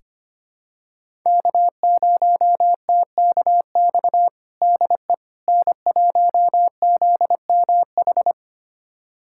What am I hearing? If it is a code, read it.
Morse code, heard as a keyed tone, about 25 wpm, 700 Hz. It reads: K0TKX DE N1ZMH
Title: USA Callsign DE Callsign - 5 Characters - 25wpm